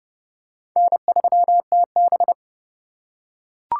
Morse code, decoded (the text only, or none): N3TB